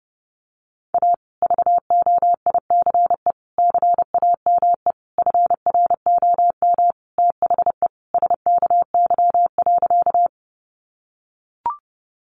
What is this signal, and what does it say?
Morse code, heard as a keyed tone, about 30 wpm, 700 Hz. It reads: A VOICE CAME FROM THE SKY.
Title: Sentences from Top 1200 Words 30wpm